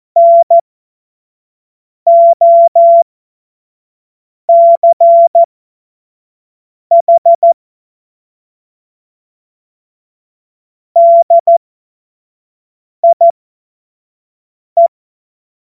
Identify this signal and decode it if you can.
Morse code, heard as a keyed tone, about 14 wpm, spaced as for 5 wpm, 680 Hz. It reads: NOCH DIE
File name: Morsekurs / Witze / Witz 19 März 2024 25 F70